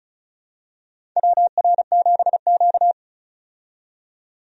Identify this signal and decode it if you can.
Morse code, heard as a keyed tone, about 36 wpm, 700 Hz. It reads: WR7Q
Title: USA Call Signs - 4 Characters - 35wpm